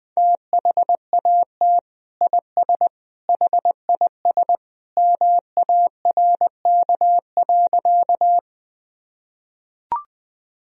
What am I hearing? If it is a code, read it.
Morse code, heard as a keyed tone, about 20 wpm, 700 Hz. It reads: THAT IS HIS MARK.
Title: Sentences from Top 1000 Words - Review Part 1 - 20wpm